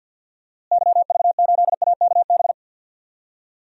Morse code, heard as a keyed tone, about 51 wpm, 700 Hz. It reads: YV8AKB
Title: DXCC-100 Call Signs - 6 Characters - Repeated 3x - 50wpm